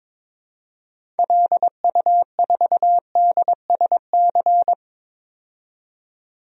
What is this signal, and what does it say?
Morse code, heard as a keyed tone, about 22 wpm, 700 Hz. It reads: LU4DSC
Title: DXCC-100 Call Signs - 6 Characters - 22wpm